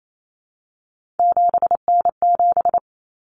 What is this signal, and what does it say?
Morse code, heard as a keyed tone, about 28 wpm, 700 Hz. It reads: 7N7
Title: Common Cut Numbers - 3-Digit 28wpm